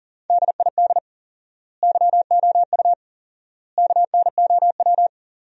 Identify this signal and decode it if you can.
Morse code, heard as a keyed tone, about 40 wpm, 700 Hz. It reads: DID YOU KNOW
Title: 4x Word-Spacing - Sentences from Top 100 Words - 40wpm